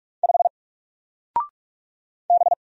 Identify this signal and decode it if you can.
Morse code, heard as a keyed tone, about 44 wpm, 700 Hz. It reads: 5 B
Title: Single Letter-Number Alt 45wpm